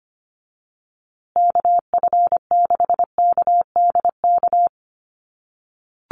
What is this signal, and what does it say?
Morse code, heard as a keyed tone, about 25 wpm, 700 Hz. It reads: KF6KDK
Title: USA Call Signs Repeated 3x 25wpm